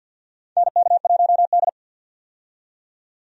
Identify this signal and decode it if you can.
Morse code, heard as a keyed tone, about 47 wpm, 700 Hz. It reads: NK1D
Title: USA Call Signs - 4 Characters - 50wpm